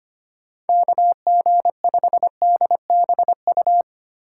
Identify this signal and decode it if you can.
Morse code, heard as a keyed tone, about 25 wpm, 700 Hz. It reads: KG5DBU
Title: USA Call Signs - 6 Characters - 2x Character Spacing - 25wpm